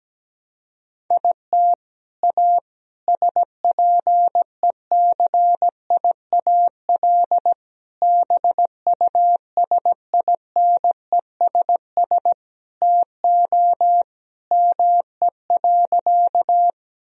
Morse code, heard as a keyed tone, about 17 wpm, 700 Hz. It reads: IT A SPECIAL BUSINESS TO ME.
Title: Sentences from Top 1000 Words - Review Part 3 - 17wpm